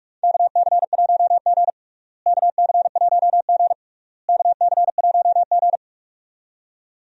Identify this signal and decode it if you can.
Morse code, heard as a keyed tone, about 41 wpm, 700 Hz. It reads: KC1G KC1G KC1G
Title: USA Call Signs - 4 Characters - Repeated 3x - 45wpm